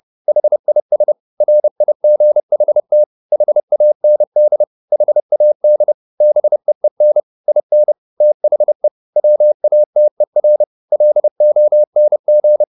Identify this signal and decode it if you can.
Morse code, heard as a keyed tone, about 30 wpm, 600 Hz. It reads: HIS RIGHT HAND HAD BEEN IN THE WATER LONG